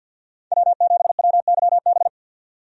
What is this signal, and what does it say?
Morse code, heard as a keyed tone, about 52 wpm, 700 Hz. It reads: W7WYB
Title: USA Call Signs - 5 Characters - 50wpm